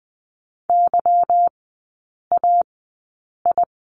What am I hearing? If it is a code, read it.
Morse code, heard as a keyed tone, about 20 wpm, spaced as for 8 wpm, 700 Hz. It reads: YAI
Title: 145 Lesson 24e - Letter y - Three Characters 20wpm